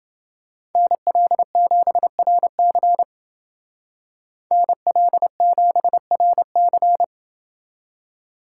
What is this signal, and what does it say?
Morse code, heard as a keyed tone, about 28 wpm, 700 Hz. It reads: NL7RC NL7RC